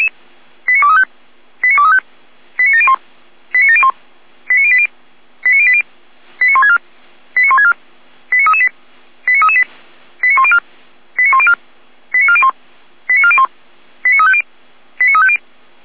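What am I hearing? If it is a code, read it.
Five-tone selective call (ZVEI), heard as quick runs of five tones, tones of about 70 ms. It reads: 89235 89235 89701 89701 89080 89080 89164 89164 89338 89338 89104 89104 89401 89401 89350 89350
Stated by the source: ZVEI five-tone selective call 89590 received on an SDR.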